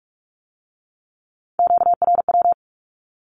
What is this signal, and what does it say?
Morse code, heard as a keyed tone, about 46 wpm, 700 Hz. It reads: QRW
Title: Q-Codes with Spoken Meaning - 45wpm